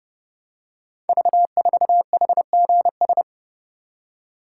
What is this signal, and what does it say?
Morse code, heard as a keyed tone, about 29 wpm, 690 Hz. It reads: V4HGS
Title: Code Groups - 5 Letter:Numbers - 30wpm